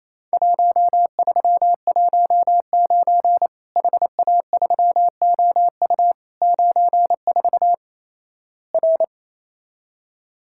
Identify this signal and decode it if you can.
Morse code, heard as a keyed tone, about 28 wpm, 700 Hz. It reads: 1319 HA3OU 94 R
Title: Worked All Europe DX Contest - QTC Exchange - 28wpm